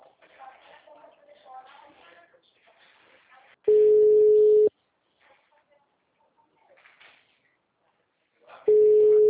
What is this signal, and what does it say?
Ringback tone (Europe, 425 Hz), with speech.